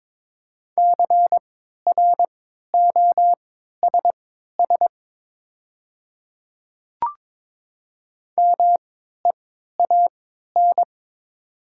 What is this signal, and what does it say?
Morse code, heard as a keyed tone, about 22 wpm, spaced as for 12 wpm, 700 Hz. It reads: CROSS MEAN
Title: 3x Character Spacing - Top 300 Words - 22wpm